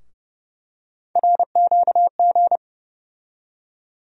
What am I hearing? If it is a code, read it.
Morse code, heard as a keyed tone, about 30 wpm, 700 Hz. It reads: RQG
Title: USA Call Sign Postfixes - 3 Letters 30wpm